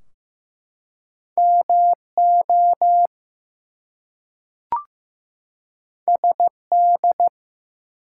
Morse code, heard as a keyed tone, about 15 wpm, 700 Hz. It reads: MO SD
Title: US States and Canadian Provinces 15wpm